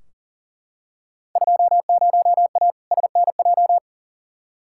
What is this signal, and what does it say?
Morse code, heard as a keyed tone, about 37 wpm, 700 Hz. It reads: 20A SNJ
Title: Field Day Exchanges Encore - 40wpm